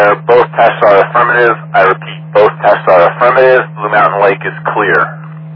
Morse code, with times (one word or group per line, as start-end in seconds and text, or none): none